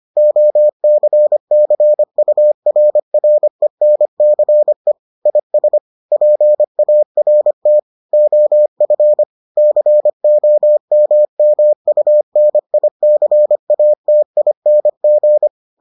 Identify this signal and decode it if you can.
Morse code, heard as a keyed tone, about 25 wpm, 600 Hz. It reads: OCCURRENCE IS PART OF COMMUNICATING